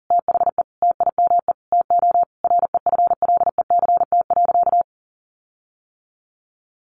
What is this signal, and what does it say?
Morse code, heard as a keyed tone, about 40 wpm, 700 Hz. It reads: THE TIME TO REFLECT.
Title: Sentences from Top 1500 Words 40wpm